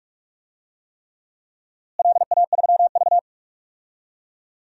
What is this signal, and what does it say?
Morse code, heard as a keyed tone, about 46 wpm, 700 Hz. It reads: LA3V
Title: DXCC-100 Call Signs - 4 Characters - Repeated 3x - 45wpm